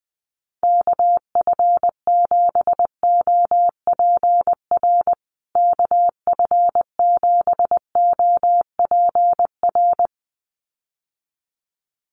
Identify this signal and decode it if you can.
Morse code, heard as a keyed tone, about 20 wpm, 700 Hz. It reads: KF7OPR KF7OPR
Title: USA Call Signs Repeated 2x 20wpm